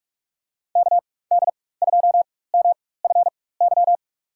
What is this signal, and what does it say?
Morse code, heard as a keyed tone, about 44 wpm, spaced as for 21 wpm, 700 Hz. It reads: KD2MFY
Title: USA Call Signs - 6 Characters - 4x Character Spacing - 45wpm